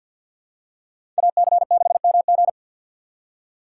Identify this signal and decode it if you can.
Morse code, heard as a keyed tone, about 48 wpm, 700 Hz. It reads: AC6MG
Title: USA Call Signs - 5 Characters - 3x Character Spacing - 50wpm